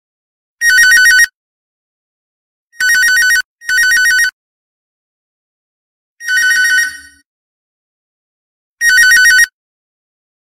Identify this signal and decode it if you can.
There is no code.